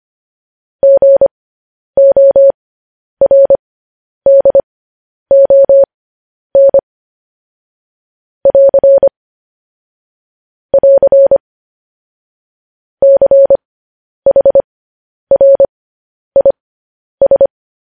Morse code, heard as a keyed tone, about 25 wpm, spaced as for 10 wpm, 550 Hz. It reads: GORDON <AR> <AR> CHRIS
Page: News Headlines in Morse Code at 10 WPM